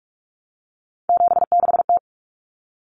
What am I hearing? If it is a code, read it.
Morse code, heard as a keyed tone, about 44 wpm, 700 Hz. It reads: 76T
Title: Common Cut Numbers - 3-Digit 45wpm